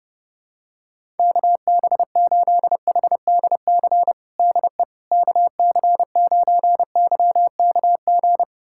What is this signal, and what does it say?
Morse code, heard as a keyed tone, about 32 wpm, 700 Hz. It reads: KB8HDC DE KC9YKG
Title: USA Callsign DE Callsign - 6 Characters - 30wpm